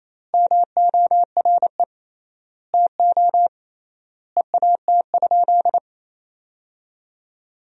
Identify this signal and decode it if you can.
Morse code, heard as a keyed tone, about 28 wpm, 700 Hz. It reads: MORE TO EAT?
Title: 3x Word-Spacing - Sentences from Top 300 Words - 28wpm